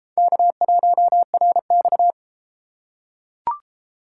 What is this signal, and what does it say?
Morse code, heard as a keyed tone, about 33 wpm, 700 Hz. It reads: K1RX